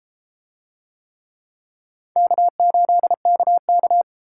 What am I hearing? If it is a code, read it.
Morse code, heard as a keyed tone, about 32 wpm, 700 Hz. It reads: K8KK